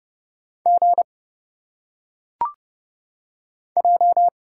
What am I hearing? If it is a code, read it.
Morse code, heard as a keyed tone, about 30 wpm, 700 Hz. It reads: G J
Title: Single Letters - Phonetic Pronunciation 30wpm